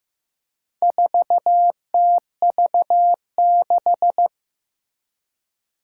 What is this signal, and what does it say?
Morse code, heard as a keyed tone, about 15 wpm, 700 Hz. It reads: 4TV6
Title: Four Challenging Letter-Number Combinations 15wpm